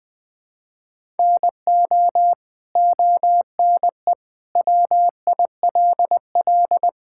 Morse code, heard as a keyed tone, about 21 wpm, 700 Hz. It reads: NO ONE WILL
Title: Sentences from Top 1000 Words - Review Part 5 - 20wpm